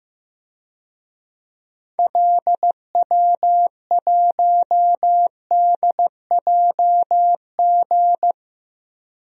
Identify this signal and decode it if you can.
Morse code, heard as a keyed tone, about 15 wpm, 700 Hz. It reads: LW1DJG
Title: DXCC-100 Call Signs - 6 Characters - Repeated 2x - 15wpm